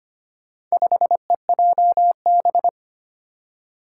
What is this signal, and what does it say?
Morse code, heard as a keyed tone, about 25 wpm, 700 Hz. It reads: 5EJB
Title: Four Challenging Letter-Number Combinations 25wpm